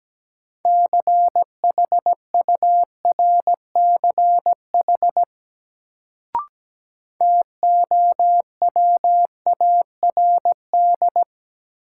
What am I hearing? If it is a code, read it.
Morse code, heard as a keyed tone, about 17 wpm, 700 Hz. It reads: CHURCH TOWARD
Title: Sets of 5 Words - New Word Drill - 17wpm